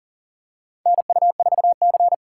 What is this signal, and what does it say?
Morse code, heard as a keyed tone, about 38 wpm, 700 Hz. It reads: NU4C
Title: USA Call Signs - 4 Characters - 4x Character Spacing - 40wpm